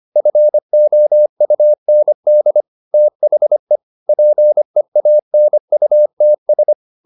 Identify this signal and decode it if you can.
Morse code, heard as a keyed tone, about 25 wpm, 600 Hz. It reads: FOUND THE PEANUTS